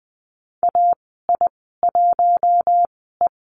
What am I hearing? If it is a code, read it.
Morse code, heard as a keyed tone, about 20 wpm, spaced as for 14 wpm, 700 Hz. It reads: AI1E
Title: 082 Lesson 14g - Number 2 - Call Signs 20wpm